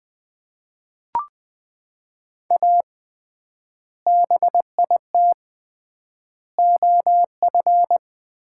Morse code, heard as a keyed tone, about 20 wpm, 700 Hz. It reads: A BIT OF
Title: 3x Word-Spacing - Sets of 4 Words - 20wpm